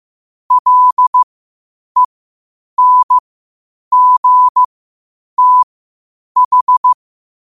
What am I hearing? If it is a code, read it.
Morse code, heard as a keyed tone, about 15 wpm, spaced as for 8 wpm, 1000 Hz. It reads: LENGTH
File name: qotdfiles / 8wpm